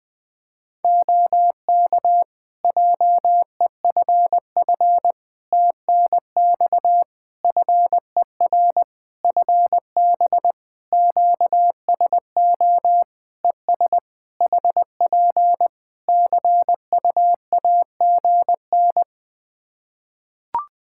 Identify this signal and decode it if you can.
Morse code, heard as a keyed tone, about 20 wpm, 700 Hz. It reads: OK JEFF TNX FER FB QSO ES HP CUAGN
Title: Practice QSO - Exchange 4 - Ending - 20wpm